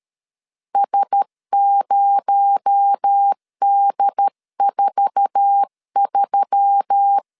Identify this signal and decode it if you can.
Morse code, heard as a keyed tone, about 13 wpm, 800 Hz. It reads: S0D43
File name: LEZIONI / 9 28 12wpm 2351